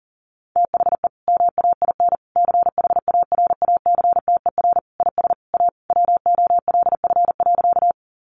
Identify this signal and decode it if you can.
Morse code, heard as a keyed tone, about 40 wpm, 700 Hz. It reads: THE MAIN CHARACTER IS A WOLF.
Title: Additional Easy Phrases 40wpm